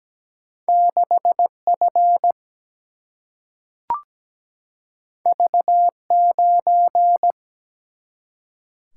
Morse code, heard as a keyed tone, about 17 wpm, 700 Hz. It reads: 6 F V 9
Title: Single-Letter Single-Number 17wpm